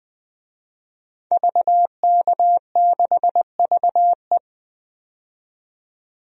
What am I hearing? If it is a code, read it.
Morse code, heard as a keyed tone, about 20 wpm, 700 Hz. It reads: VK6VE
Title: DXCC-100 Call Signs - 5 Characters - 20wpm